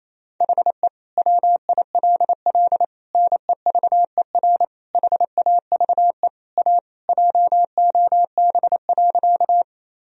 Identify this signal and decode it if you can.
Morse code, heard as a keyed tone, about 28 wpm, 700 Hz. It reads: HE WILL NEVER HAVE A JOB.
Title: Sentences from Top 1000 Words - Review Part 3 - 28wpm